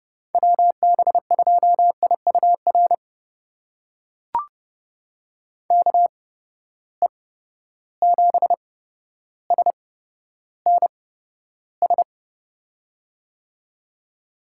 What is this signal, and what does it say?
Morse code, heard as a keyed tone, about 32 wpm, spaced as for 13 wpm, 700 Hz. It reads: WB2IUR KE7SNS
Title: USA Call Signs - 6 Characters - 8x Character Spacing - 30wpm